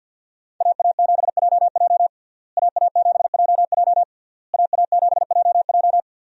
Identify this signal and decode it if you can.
Morse code, heard as a keyed tone, about 53 wpm, 700 Hz. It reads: AA7JJ AA7JJ AA7JJ
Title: USA Call Signs - 5 Characters - Repeated 3x - 50wpm